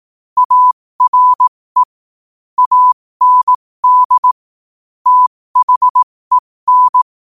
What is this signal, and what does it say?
Morse code, heard as a keyed tone, about 18 wpm, spaced as for 15 wpm, 1000 Hz. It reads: ARE AND THEN
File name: qotdfiles / 15wpm